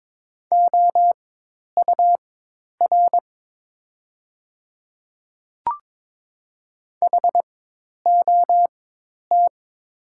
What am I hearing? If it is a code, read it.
Morse code, heard as a keyed tone, about 22 wpm, spaced as for 10 wpm, 700 Hz. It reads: OUR HOT
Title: 4x Character Spacing - Top 300 Words - 22wpm